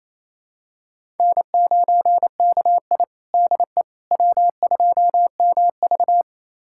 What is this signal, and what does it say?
Morse code, heard as a keyed tone, about 28 wpm, 700 Hz. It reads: N9KI DE W2MV
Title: USA Callsign DE Callsign - 4 Characters - 28wpm